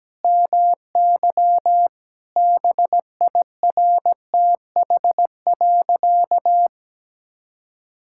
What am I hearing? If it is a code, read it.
Morse code, heard as a keyed tone, about 17 wpm, 700 Hz. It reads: MY BIRTH.
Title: Sentences from Top 1000 Words - Review Part 5 - 17wpm